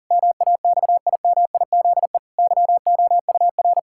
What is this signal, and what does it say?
Morse code, heard as a keyed tone, about 40 wpm, 700 Hz. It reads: MAXIMIZE YOUR
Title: Pangram Challenge v2 - 40wpm